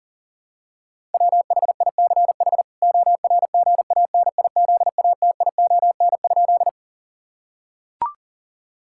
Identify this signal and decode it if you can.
Morse code, heard as a keyed tone, about 40 wpm, 700 Hz. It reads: WHICH ORGANIZATION?
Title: Sentences from Top 1000 Words - Review Part 5 - 40wpm